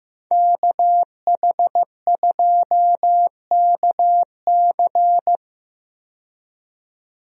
Morse code, heard as a keyed tone, about 15 wpm, 700 Hz. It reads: KH2KC
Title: USA Call Signs - 5 Characters - 15wpm